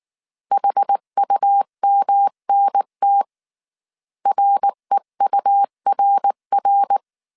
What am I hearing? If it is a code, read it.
Morse code, heard as a keyed tone, about 19 wpm, 800 Hz. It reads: HUMNT REURR
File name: CW / DET / 1 3 1500C S B 18wpm 1916